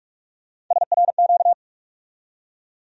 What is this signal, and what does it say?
Morse code, heard as a keyed tone, about 46 wpm, 700 Hz. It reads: SRQ
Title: USA Call Sign Postfixes - 3 Letters 45wpm